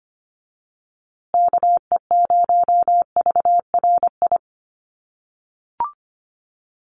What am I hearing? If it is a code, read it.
Morse code, heard as a keyed tone, about 25 wpm, 700 Hz. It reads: KE0VRI
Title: USA Call Signs Repeated 2x 25wpm